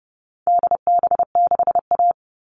Morse code, heard as a keyed tone, about 30 wpm, 700 Hz. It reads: DB6A
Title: All Cut Numbers - 4-Digit 30wpm